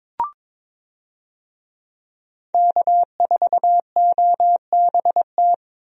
Morse code, heard as a keyed tone, about 22 wpm, 700 Hz. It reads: K4OBT
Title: USA Call Signs - 5 Characters - Repeated 3x - 22wpm